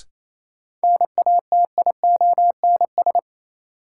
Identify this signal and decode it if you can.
Morse code, heard as a keyed tone, about 28 wpm, 700 Hz. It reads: NATIONS
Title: Sets of 3 Words - New Word Drill - 28wpm